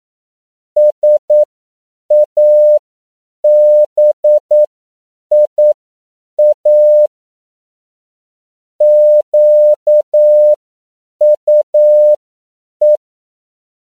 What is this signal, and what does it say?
Morse code, heard as a keyed tone, about 9 wpm, spaced as for 7 wpm, 600 Hz. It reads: SABIA QUE
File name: CW / 10PPM